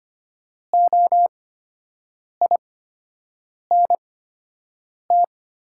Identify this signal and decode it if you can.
Morse code, heard as a keyed tone, about 25 wpm, spaced as for 7 wpm, 700 Hz. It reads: OINT
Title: 8x Character Spacing - Top 200 Words - 25wpm